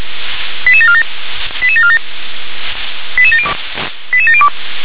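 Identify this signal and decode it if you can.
Five-tone selective call (ZVEI), heard as quick runs of five tones, tones of about 70 ms. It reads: 88747 88747 88702
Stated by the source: Five-tone ZVEI selective call received on SDR.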